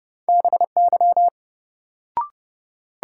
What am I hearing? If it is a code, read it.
Morse code, heard as a keyed tone, about 30 wpm, 700 Hz. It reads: BY
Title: Sentences from Top 100 Words - New Word Drill - 30wpm